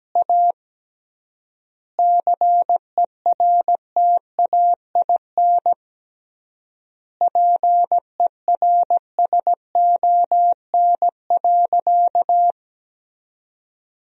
Morse code, heard as a keyed tone, about 17 wpm, 700 Hz. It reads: A CERTAIN PERSON.
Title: 3x Word-Spacing - Sentences from Top 500 Words - 17wpm